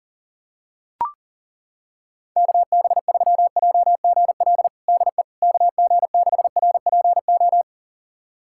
Morse code, heard as a keyed tone, about 40 wpm, 700 Hz. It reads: KB3JGL DE KG6RPO
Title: USA Callsign DE Callsign 40wpm